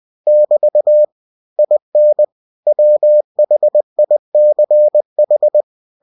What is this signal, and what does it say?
Morse code, heard as a keyed tone, about 20 wpm, 600 Hz. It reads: <BT> IN WHICH
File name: cw-books / winnie-the-pooh / winnie-20wpm-page-0033